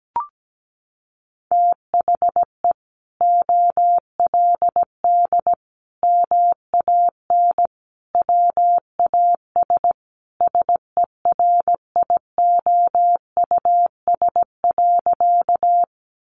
Morse code, heard as a keyed tone, about 17 wpm, 700 Hz. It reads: THE OLD MAN WAS SERIOUS.
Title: Sentences from Top 1100 Words 17wpm